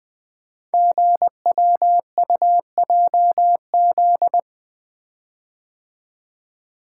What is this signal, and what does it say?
Morse code, heard as a keyed tone, about 20 wpm, 700 Hz. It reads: GWUJZ
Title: Code Groups - 5 Letters - 20wpm